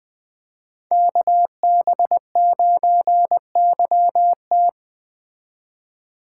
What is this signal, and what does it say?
Morse code, heard as a keyed tone, about 20 wpm, 700 Hz. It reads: KB9YT
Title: Speed Racing - USA 2x2 Call Signs 17wpm